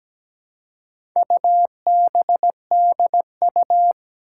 Morse code, heard as a keyed tone, about 17 wpm, 700 Hz. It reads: UBDU
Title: Four Challenging Letter-Number Combinations 17wpm